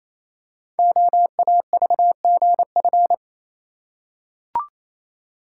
Morse code, heard as a keyed tone, about 27 wpm, 700 Hz. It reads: OAVGF
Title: Code Groups - 5 Letters - 28wpm